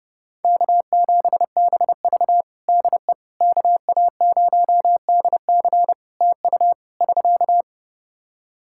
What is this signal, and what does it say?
Morse code, heard as a keyed tone, about 30 wpm, 700 Hz. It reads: K7BV DE KA0DC TU <SK>
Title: Practice QSO - Exchange 4 - Ending - 30wpm